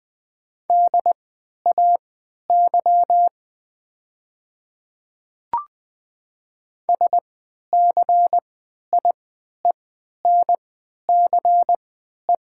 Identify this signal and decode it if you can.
Morse code, heard as a keyed tone, about 20 wpm, spaced as for 11 wpm, 700 Hz. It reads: DAY SCIENCE
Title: 3x Character Spacing - Top 500 Words - 20wpm